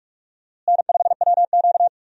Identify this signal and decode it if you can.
Morse code, heard as a keyed tone, about 44 wpm, 700 Hz. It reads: N5WQ
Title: USA Call Signs - 4 Characters - Repeated 2x - 45wpm